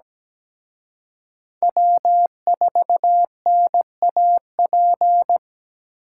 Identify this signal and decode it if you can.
Morse code, heard as a keyed tone, about 20 wpm, 700 Hz. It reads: W4NAP
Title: Speed Racing - USA 1x3 Call Signs 17wpm